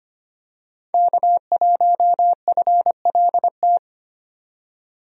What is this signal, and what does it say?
Morse code, heard as a keyed tone, about 25 wpm, 700 Hz. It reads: K1FLT
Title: USA Call Signs - 5 Characters - 25wpm